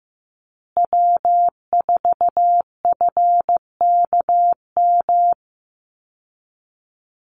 Morse code, heard as a keyed tone, about 15 wpm, 700 Hz. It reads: W4FKM
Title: Call Signs Repeated 3x - 15wpm